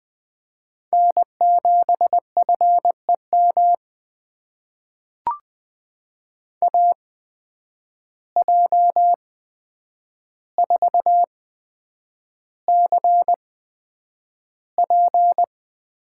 Morse code, heard as a keyed tone, about 20 wpm, spaced as for 9 wpm, 700 Hz. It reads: N7FEM AJ4CP
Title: USA Call Signs - 5 Characters - 8x Character Spacing - 20wpm